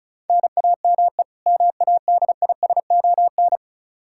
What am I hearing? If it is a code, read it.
Morse code, heard as a keyed tone, about 36 wpm, 700 Hz. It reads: NAME MADISON
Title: Practice QSO - Exchange 1 - Beginning - 35wpm